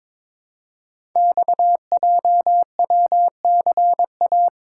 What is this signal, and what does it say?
Morse code, heard as a keyed tone, about 22 wpm, 700 Hz. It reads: XJWCA
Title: Code Groups - 5 Letter:Numbers - 22wpm